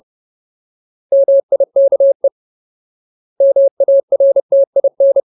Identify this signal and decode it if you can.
Morse code, heard as a keyed tone, about 30 wpm, 550 Hz. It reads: MIKE MARTIN
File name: Ultimate Word List / 30WPM Ultimate